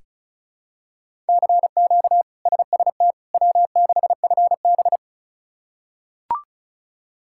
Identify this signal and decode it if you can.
Morse code, heard as a keyed tone, about 34 wpm, 700 Hz. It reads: CQ SST W6FB
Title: K1USN SST QSO Practice - 35wpm